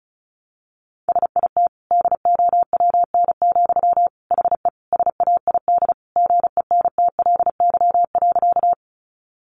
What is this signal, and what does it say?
Morse code, heard as a keyed tone, about 34 wpm, 700 Hz. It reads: SIT DOWN, HE SAID GENTLY.
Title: Sentences from Top 1500 Words 35wpm